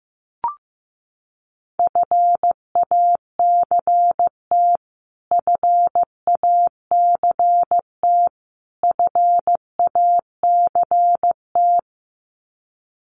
Morse code, heard as a keyed tone, about 15 wpm, 700 Hz. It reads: FACT FACT FACT
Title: Top 500 Words Repeated 15wpm